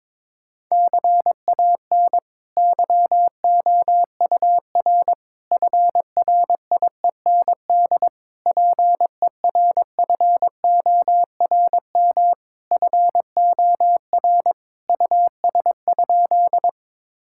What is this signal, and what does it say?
Morse code, heard as a keyed tone, about 22 wpm, 700 Hz. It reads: CAN YOUR FRIEND PERFORM FOR US?
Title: Sentences from Top 1000 Words - Review Part 5 - 22wpm